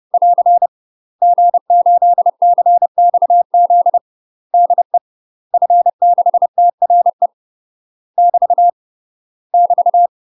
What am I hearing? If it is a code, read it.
Morse code, heard as a keyed tone, about 30 wpm, 700 Hz. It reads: <AR> G8CXZ DE F6TRE <BT> <BT>
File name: UK MASTER QSOs TRAINING / 30WPM / UK MASTER QSOs 0030